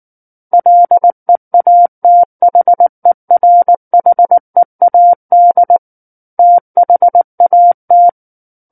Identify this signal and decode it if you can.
Morse code, heard as a keyed tone, about 19 wpm, 700 Hz. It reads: LEATHERHEAD THAT
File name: Book One / 10-Chapter 10 - 19 WPM